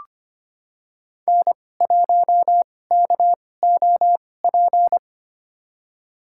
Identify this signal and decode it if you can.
Morse code, heard as a keyed tone, about 25 wpm, spaced as for 18 wpm, 700 Hz. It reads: N1KOP
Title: USA Call Signs - 2x Character Spacing - 25wpm